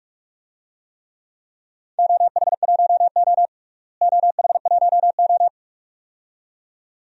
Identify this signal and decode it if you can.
Morse code, heard as a keyed tone, about 41 wpm, 700 Hz. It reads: OH1O OH1O
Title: DXCC-100 Call Signs - 4 Characters - Repeated 2x - 45wpm